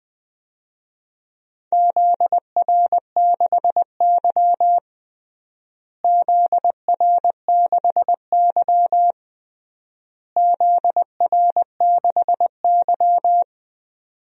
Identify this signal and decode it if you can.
Morse code, heard as a keyed tone, about 20 wpm, 700 Hz. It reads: ZR6Y ZR6Y ZR6Y
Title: DXCC-100 Call Signs - 4 Characters - Repeated 3x - 20wpm